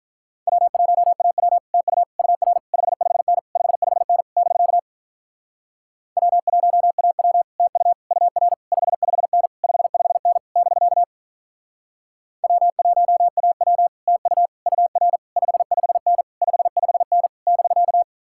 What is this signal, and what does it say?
Morse code, heard as a keyed tone, about 45 wpm, 700 Hz. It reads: W1AW TU UR 55N 55N <BK> W1AW TU UR 55N 55N <BK> W1AW TU UR 55N 55N <BK>